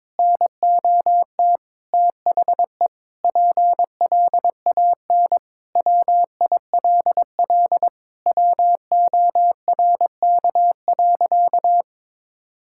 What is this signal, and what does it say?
Morse code, heard as a keyed tone, about 22 wpm, 700 Hz. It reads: NOT THE PLAN WILL WORK.
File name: practice / 22wpm / Sentences with 8-Word Phrases - 22wpm